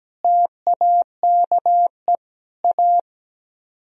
Morse code, heard as a keyed tone, about 17 wpm, 700 Hz. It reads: TAKE A
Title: Speed-Racing - Sets of 4 Words 17wpm